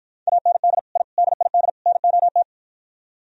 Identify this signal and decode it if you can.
Morse code, heard as a keyed tone, about 53 wpm, 700 Hz. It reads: AND I DID NOT